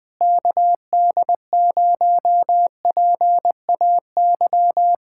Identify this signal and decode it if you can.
Morse code, heard as a keyed tone, about 20 wpm, 700 Hz. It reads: KD0PAY
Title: Speed-Racing Common QSO Phrases Encore - 17wpm